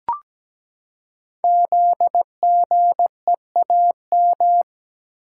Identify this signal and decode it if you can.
Morse code, heard as a keyed tone, about 17 wpm, 700 Hz. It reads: ZGEAM
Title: Code Groups - 5 Letters - 17wpm